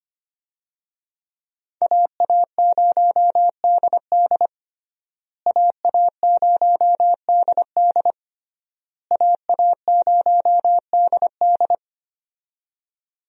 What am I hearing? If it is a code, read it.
Morse code, heard as a keyed tone, about 25 wpm, 700 Hz. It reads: AA0DD AA0DD AA0DD
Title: USA Call Signs - 5 Characters - Repeated 3x - 25wpm